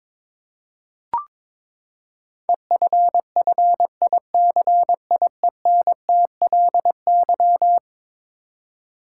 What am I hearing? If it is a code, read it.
Morse code, heard as a keyed tone, about 22 wpm, 700 Hz. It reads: EFFICIENTLY
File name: practice / 22wpm / Eleven Letter Words - 22wpm